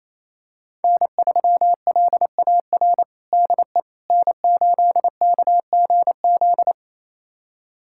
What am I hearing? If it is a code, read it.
Morse code, heard as a keyed tone, about 27 wpm, 700 Hz. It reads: N3LAR DE N8KGZ
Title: USA Callsign DE Callsign - 5 Characters - 28wpm